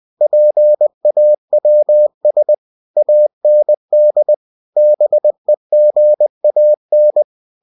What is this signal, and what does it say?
Morse code, heard as a keyed tone, about 20 wpm, 600 Hz. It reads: PAWS AND BEGAN
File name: cw-books / winnie-the-pooh / winnie-20wpm-page-0002